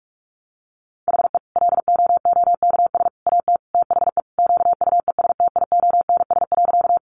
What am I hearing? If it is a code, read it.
Morse code, heard as a keyed tone, about 44 wpm, 700 Hz. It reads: HE LOOKS AT THE QUESTIONS.
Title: Additional Easy Phrases 45wpm